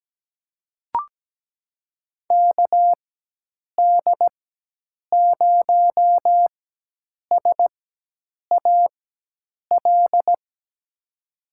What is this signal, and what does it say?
Morse code, heard as a keyed tone, about 17 wpm, spaced as for 8 wpm, 700 Hz. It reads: KD0SAL
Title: USA Call Signs - 6 Characters - 4x Character Spacing - 17wpm